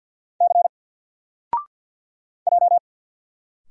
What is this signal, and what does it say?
Morse code, heard as a keyed tone, about 50 wpm, 700 Hz. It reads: C J
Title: Single Letter-Number 50wpm